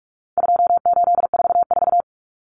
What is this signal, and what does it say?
Morse code, heard as a keyed tone, about 44 wpm, 700 Hz. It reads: 2844
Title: Numbers - 4-Digit 45wpm